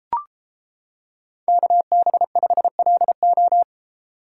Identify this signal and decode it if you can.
Morse code, heard as a keyed tone, about 30 wpm, 700 Hz. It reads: KB5LO